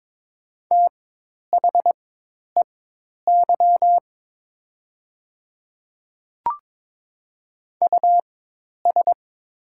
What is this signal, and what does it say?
Morse code, heard as a keyed tone, about 22 wpm, spaced as for 10 wpm, 700 Hz. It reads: THEY US
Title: 4x Character Spacing - Top 500 Words - 22wpm